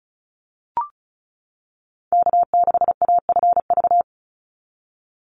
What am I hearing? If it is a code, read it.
Morse code, heard as a keyed tone, about 34 wpm, 700 Hz. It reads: K6AFV
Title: USA Call Signs 35wpm